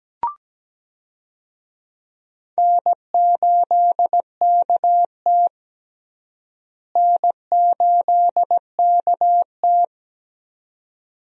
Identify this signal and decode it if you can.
Morse code, heard as a keyed tone, about 17 wpm, 700 Hz. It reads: N8KT N8KT
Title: USA Call Signs - 4 Characters - Repeated 2x - 17wpm